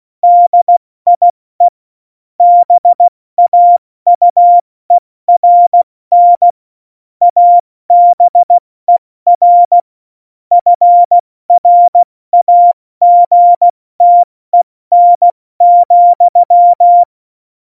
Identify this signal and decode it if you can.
Morse code, heard as a keyed tone, about 16 wpm, 700 Hz. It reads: DIE BAUERN ABER FRAGTEN,
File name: Morsekurs / Witze / Witz 21 Juli 2020, 70 F80